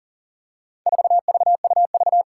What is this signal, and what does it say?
Morse code, heard as a keyed tone, about 40 wpm, 700 Hz. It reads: 4VUV
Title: Four Challenging Letter-Number Combinations 40wpm